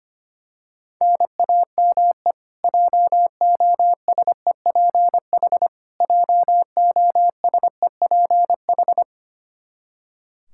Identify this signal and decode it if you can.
Morse code, heard as a keyed tone, about 25 wpm, 700 Hz. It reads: NAME JOSEPH JOSEPH
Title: K1USN SST QSO Practice 25wpm